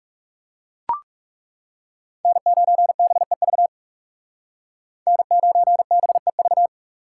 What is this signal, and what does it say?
Morse code, heard as a keyed tone, about 39 wpm, 700 Hz. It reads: N9BEV N9BEV